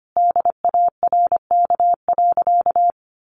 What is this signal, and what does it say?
Morse code, heard as a keyed tone, about 25 wpm, 700 Hz. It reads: DARK.
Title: Sentences from Top 1400 Words 25wpm